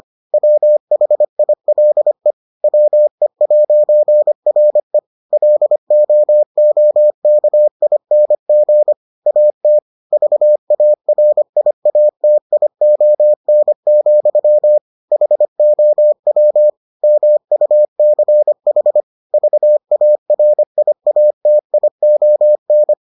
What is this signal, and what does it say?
Morse code, heard as a keyed tone, about 25 wpm, 600 Hz. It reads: WHILE WE'RE LOOKING AT VARIATION, HOW MUCH VARIATION